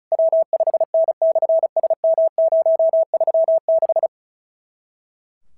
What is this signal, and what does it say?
Morse code, heard as a keyed tone, about 36 wpm, 650 Hz. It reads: W5N/SM036
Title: SOTA QSO Practice 35wpm